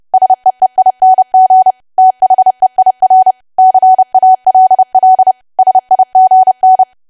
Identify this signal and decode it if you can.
Morse code, heard as a keyed tone, about 30 wpm, 750 Hz. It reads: SEEING THEIR CALL SIGN
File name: cwa / W1AW-Level3 / 30WPM